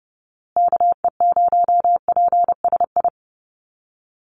Morse code, heard as a keyed tone, about 30 wpm, 700 Hz. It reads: KE0PSI
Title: USA Call Signs Repeated 2x 30wpm